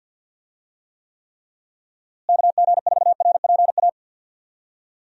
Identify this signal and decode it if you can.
Morse code, heard as a keyed tone, about 52 wpm, 700 Hz. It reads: KG4RPA
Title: USA Call Signs - 6 Characters - Repeated 3x - 50wpm